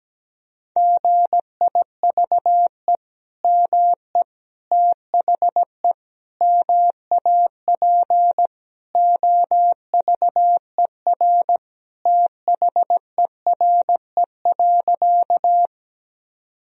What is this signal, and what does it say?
Morse code, heard as a keyed tone, about 17 wpm, 700 Hz. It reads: GIVE ME THE MAP OVER THERE.
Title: Sentences from Top 1000 Words - Review Part 2 - 17wpm